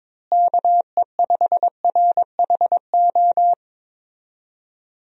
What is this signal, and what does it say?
Morse code, heard as a keyed tone, about 22 wpm, 700 Hz. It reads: KE5RHO
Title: USA Call Signs - 6 Characters - Repeated 3x - 22wpm